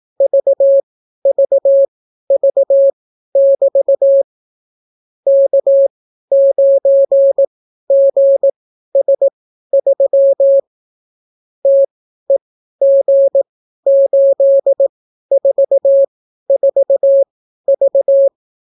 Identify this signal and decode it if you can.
Morse code, heard as a keyed tone, about 18 wpm, spaced as for 12 wpm, 550 Hz. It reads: VVV<BT> K9GS3 TEG844V